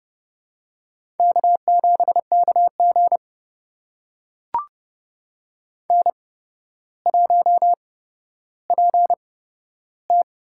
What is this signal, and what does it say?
Morse code, heard as a keyed tone, about 30 wpm, spaced as for 13 wpm, 700 Hz. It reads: K7KG N1PT
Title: USA Call Signs - 4 Characters - 8x Character Spacing - 30wpm